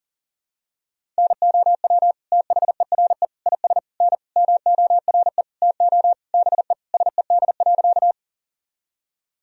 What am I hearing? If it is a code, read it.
Morse code, heard as a keyed tone, about 40 wpm, 700 Hz. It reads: NOW THERE IS N MORE TO BE SED.
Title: Sentences from Top 100 Words and Abbreviations 40wpm